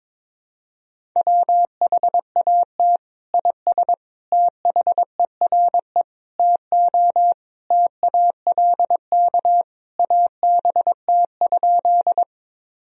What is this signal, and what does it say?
Morse code, heard as a keyed tone, about 22 wpm, 700 Hz. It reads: WHAT IS THERE TO TALK ABT?
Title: Sentences from Top 400 Words and Abbreviations 22wpm